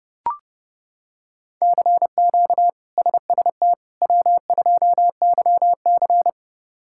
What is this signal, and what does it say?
Morse code, heard as a keyed tone, about 30 wpm, 700 Hz. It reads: CQ SST W2YC
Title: K1USN SST QSO Practice - 30wpm